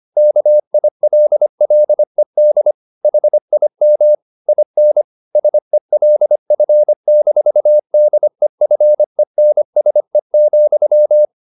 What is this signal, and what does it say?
Morse code, heard as a keyed tone, about 25 wpm, 600 Hz. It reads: KILLED HIM IN SELF-DEFENSE,